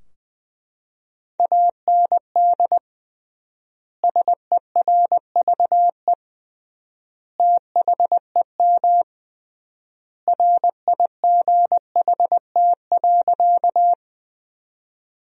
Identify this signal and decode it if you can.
Morse code, heard as a keyed tone, about 20 wpm, 700 Hz. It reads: AND SERVE THEM RIGHT.
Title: 3x Word-Spacing - Sentences from Top 400 Words - 20wpm